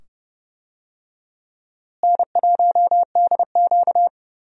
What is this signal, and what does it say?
Morse code, heard as a keyed tone, about 30 wpm, 700 Hz. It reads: N1DQ
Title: USA Call Signs - 4 Characters - Repeated 2x - 30wpm